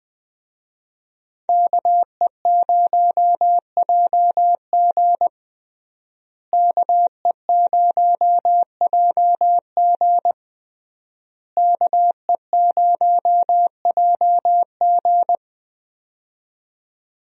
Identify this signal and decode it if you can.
Morse code, heard as a keyed tone, about 19 wpm, 700 Hz. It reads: KE0JG KE0JG KE0JG
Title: USA Call Signs - 5 Characters - Repeated 3x - 20wpm